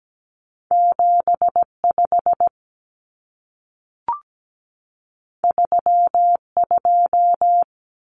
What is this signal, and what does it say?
Morse code, heard as a keyed tone, about 17 wpm, 700 Hz. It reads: 75 32
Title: Common Cut Numbers - 2-Digit 17wpm